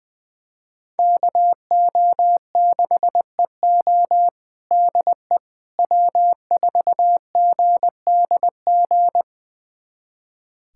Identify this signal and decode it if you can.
Morse code, heard as a keyed tone, about 21 wpm, 700 Hz. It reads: KO6EO DE W4GDG
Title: USA Callsign DE Callsign - 5 Characters - 20wpm